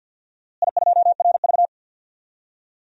Morse code, heard as a keyed tone, about 49 wpm, 700 Hz. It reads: I2RV
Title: DXCC-100 Call Signs - 4 Characters - Repeated 2x - 50wpm